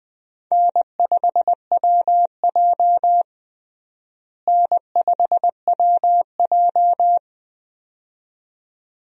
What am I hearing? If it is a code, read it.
Morse code, heard as a keyed tone, about 20 wpm, 700 Hz. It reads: N5WJ N5WJ
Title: USA Call Signs - 4 Characters - Repeated 2x - 20wpm